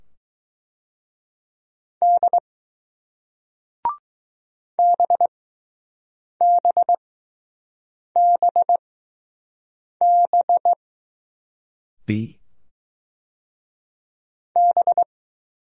Morse code, heard as a keyed tone, about 20 wpm, spaced as for 6 wpm, 700 Hz. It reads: D BBBB B